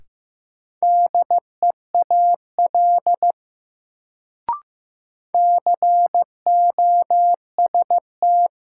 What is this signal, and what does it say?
Morse code, heard as a keyed tone, about 15 wpm, 700 Hz. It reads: DEAL COST
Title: Sets of 3 Words - New Word Drill - 15wpm